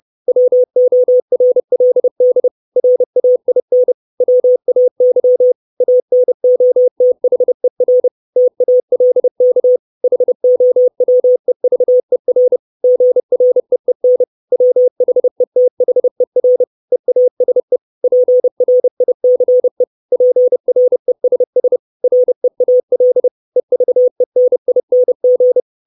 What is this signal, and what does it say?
Morse code, heard as a keyed tone, about 30 wpm, 500 Hz. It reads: WORLD RAIN WAY ANOTHER TALK HOWEVER GREEN WHETHER EASE PRICE PRESS REAL EVENING
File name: 632 Words 30WPM